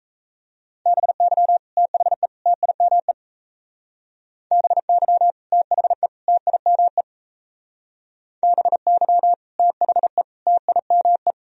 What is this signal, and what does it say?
Morse code, heard as a keyed tone, about 35 wpm, 700 Hz. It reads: BY THE TIME BY THE TIME BY THE TIME
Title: Speed-Racing - Sets of 3 Words 28wpm